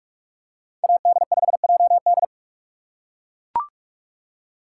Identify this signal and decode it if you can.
Morse code, heard as a keyed tone, about 42 wpm, 700 Hz. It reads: AD5JD
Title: USA Call Signs - 5 Characters - 3x Character Spacing - 45wpm